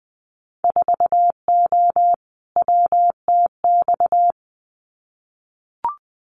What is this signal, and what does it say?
Morse code, heard as a keyed tone, about 19 wpm, 700 Hz. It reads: 4O WTX
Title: Winter Field Day Exchanges Encore 20wpm